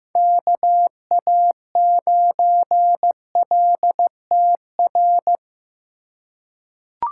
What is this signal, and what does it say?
Morse code, heard as a keyed tone, about 15 wpm, 700 Hz. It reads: KA9LTR
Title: USA Call Signs - 2x Character Spacing - 15wpm